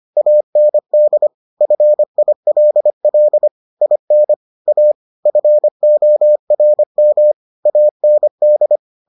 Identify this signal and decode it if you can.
Morse code, heard as a keyed tone, about 25 wpm, 600 Hz. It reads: AND FILL IN A FORM AND